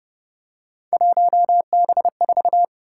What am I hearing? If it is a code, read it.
Morse code, heard as a keyed tone, about 29 wpm, 700 Hz. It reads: 1B4
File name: Speed-Racing - Three Letter-Number Combinations - 25wpm